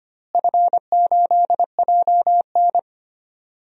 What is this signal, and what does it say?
Morse code, heard as a keyed tone, about 25 wpm, 700 Hz. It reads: F8JN
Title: DXCC-100 Call Signs - 4 Characters - Repeated 3x - 25wpm